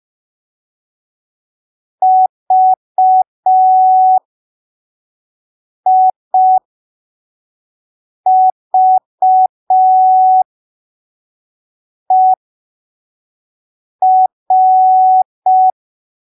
Morse code, heard as a keyed tone, about 5 wpm, spaced as for 3 wpm, 750 Hz. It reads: VIVER
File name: arquivos / texto / texto 19 05ppm